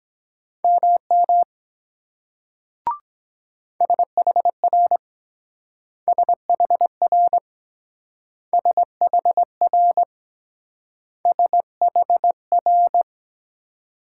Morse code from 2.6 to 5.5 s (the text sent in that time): SHR